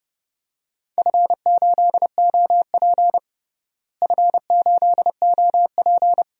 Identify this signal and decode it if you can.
Morse code, heard as a keyed tone, about 30 wpm, 700 Hz. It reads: F8OP F8OP
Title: DXCC-100 Call Signs - 4 Characters - Repeated 3x - 30wpm